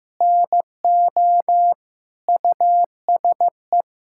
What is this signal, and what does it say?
Morse code, heard as a keyed tone, about 16 wpm, 700 Hz. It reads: NO USE
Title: Sentences from Top 1000 Words - Review Part 6 - 15wpm